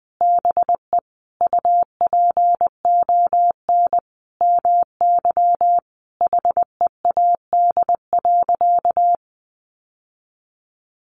Morse code, heard as a keyed tone, about 20 wpm, 700 Hz. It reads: BE UPON MY HEAD.
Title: Sentences from Top 1100 Words 20wpm